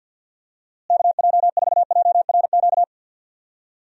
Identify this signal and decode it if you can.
Morse code, heard as a keyed tone, about 53 wpm, 700 Hz. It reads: KJ4JRQ
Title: USA Call Signs - 6 Characters - Repeated 2x - 50wpm